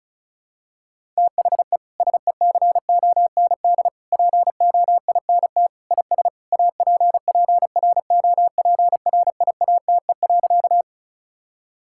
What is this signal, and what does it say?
Morse code, heard as a keyed tone, about 34 wpm, 700 Hz. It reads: THE SECOND POINT IS APPROPRIATE.
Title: Sentences from Top 1000 Words - Review Part 5 - 35wpm